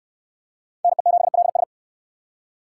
Morse code, heard as a keyed tone, about 69 wpm, 700 Hz. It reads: NE7LS